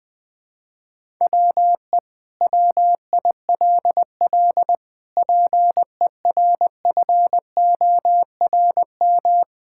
Morse code, heard as a keyed tone, about 21 wpm, 700 Hz. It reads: WE WILL PERFORM
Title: Sentences from Top 1000 Words - Review Part 5 - 20wpm